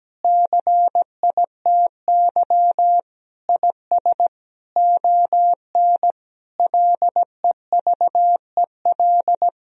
Morse code, heard as a keyed tone, about 17 wpm, 700 Hz. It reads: CITY IS ON LEVEL
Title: Sentences from Top 1000 Words - Review Part 4 - 17wpm